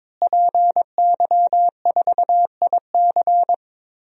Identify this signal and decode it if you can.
Morse code, heard as a keyed tone, about 22 wpm, 700 Hz. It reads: PY4IC
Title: DXCC-100 Call Signs - 5 Characters - Repeated 2x - 22wpm